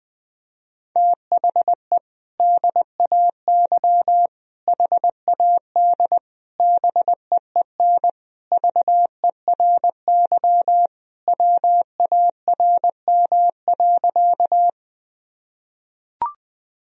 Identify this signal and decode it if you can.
Morse code, heard as a keyed tone, about 21 wpm, 700 Hz. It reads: THE DAY HAD BEEN VERY WARM.
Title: Sentences from Top 1000 Words - Review Part 2 - 20wpm